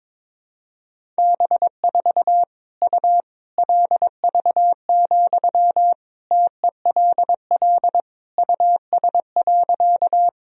Morse code, heard as a keyed tone, about 22 wpm, 700 Hz. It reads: B4 U LV, TELL US.
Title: Sentences from Top 400 Words and Abbreviations 22wpm